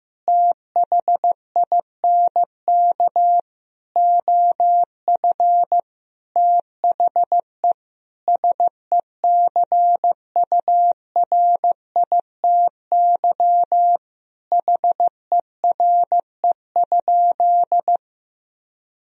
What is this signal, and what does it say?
Morse code, heard as a keyed tone, about 16 wpm, 700 Hz. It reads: THINK OF THE SECURITY HERE?
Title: Sentences from Top 1000 Words - Review Part 4 - 15wpm